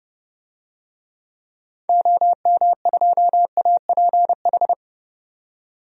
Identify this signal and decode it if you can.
Morse code, heard as a keyed tone, about 32 wpm, 700 Hz. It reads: OM2APH
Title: DXCC-100 Call Signs - 6 Characters - 30wpm